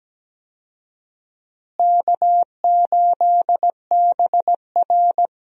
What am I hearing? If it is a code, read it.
Morse code, heard as a keyed tone, about 17 wpm, 700 Hz. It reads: K8BR
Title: USA Call Signs - 4 Characters - Repeated 3x - 17wpm